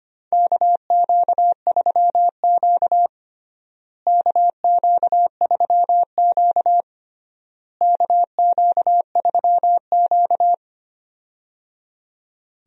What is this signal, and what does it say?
Morse code, heard as a keyed tone, about 25 wpm, 700 Hz. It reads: KQ3Q KQ3Q KQ3Q
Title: USA Call Signs - 4 Characters - Repeated 3x - 25wpm